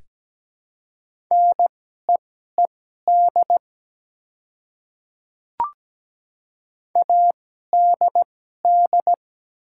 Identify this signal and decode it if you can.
Morse code, heard as a keyed tone, about 17 wpm, spaced as for 12 wpm, 700 Hz. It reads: NEED ADD
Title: 2x Character Spacing - Top 200 Words - 17wpm